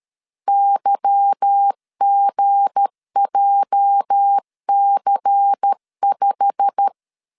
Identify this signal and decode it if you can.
Morse code, heard as a keyed tone, about 13 wpm, 800 Hz. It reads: YGJC5
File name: LEZIONI / 10 SIGNS / 10A SLASH 12wpm 2314